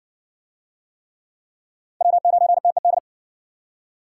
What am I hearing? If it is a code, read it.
Morse code, heard as a keyed tone, about 52 wpm, 700 Hz. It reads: W9TED